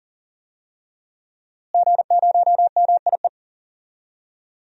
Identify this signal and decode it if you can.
Morse code, heard as a keyed tone, about 37 wpm, 700 Hz. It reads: G0MIE
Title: DXCC-100 Call Signs - 5 Characters - Repeated 3x - 40wpm